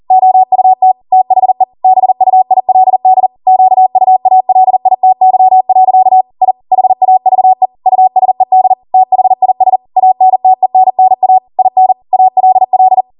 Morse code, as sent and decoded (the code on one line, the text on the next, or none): --- ..- - / - .... / -... ..- .. .-.. -.. / --.- ..- .- .-.. .. - -.-- .-.-.- / .. / .... .- ...- . / ..- ... -.. / - .... .. ... / .- -. - . -. -. .- / .. -. / .- .-.. .-..
OUT THE BUILD QUALITY. I HAVE USED THIS ANTENNA IN ALL